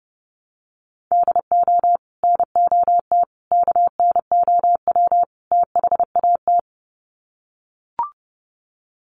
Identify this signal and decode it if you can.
Morse code, heard as a keyed tone, about 30 wpm, 700 Hz. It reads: DO NOT KNOW THAT
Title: Sets of 4 Words 30wpm